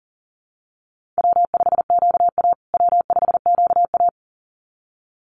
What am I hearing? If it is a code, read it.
Morse code, heard as a keyed tone, about 40 wpm, 700 Hz. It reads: W5QA W5QA
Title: USA Call Signs Repeated 2x 40wpm